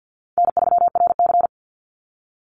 Morse code, heard as a keyed tone, about 50 wpm, 700 Hz. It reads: N3RC
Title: Call Signs Repeated 3x - 50wpm